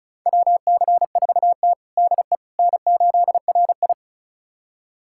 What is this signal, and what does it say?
Morse code, heard as a keyed tone, about 36 wpm, 700 Hz. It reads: WC4T DE N8RI
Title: USA Callsign DE Callsign - 4 Characters - 35wpm